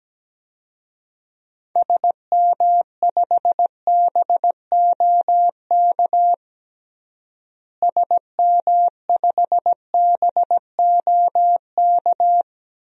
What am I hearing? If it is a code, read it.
Morse code, heard as a keyed tone, about 17 wpm, 700 Hz. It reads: SM5BOK SM5BOK
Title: DXCC-100 Call Signs - 6 Characters - Repeated 3x - 17wpm